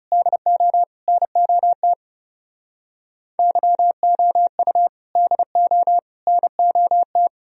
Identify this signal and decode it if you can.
Morse code, heard as a keyed tone, about 38 wpm, 700 Hz. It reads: DO NOT YOU DO NOT